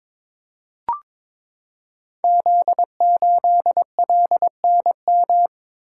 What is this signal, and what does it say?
Morse code, heard as a keyed tone, about 23 wpm, 700 Hz. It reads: Z8LNM
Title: Code Groups - 5 Letter:Numbers - 22wpm